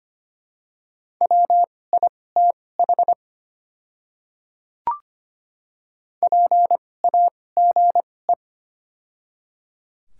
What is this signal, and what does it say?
Morse code, heard as a keyed tone, about 25 wpm, spaced as for 18 wpm, 700 Hz. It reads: WITH PAGE
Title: 2x Character Spacing - Top 300 Words - 25wpm